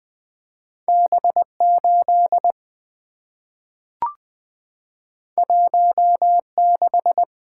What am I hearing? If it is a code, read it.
Morse code, heard as a keyed tone, about 20 wpm, 700 Hz. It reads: B 8 1 6
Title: Two Letter-Number Combinations - 20wpm